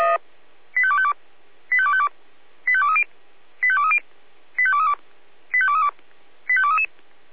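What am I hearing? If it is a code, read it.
Five-tone selective call (ZVEI), heard as quick runs of five tones, tones of about 70 ms. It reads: 85252 85252 85230 85230 85232 85232 85233
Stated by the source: ZVEI five-tone call received by SDR.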